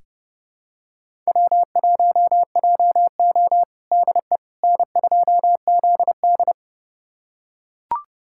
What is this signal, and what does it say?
Morse code, heard as a keyed tone, about 32 wpm, 700 Hz. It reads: W1JO DE N2ZD
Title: USA Callsign DE Callsign - 4 Characters - 30wpm